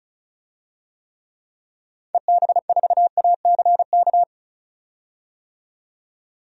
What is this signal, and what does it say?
Morse code, heard as a keyed tone, about 36 wpm, 700 Hz. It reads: EB4ACK
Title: DXCC-100 Call Signs - 6 Characters - 35wpm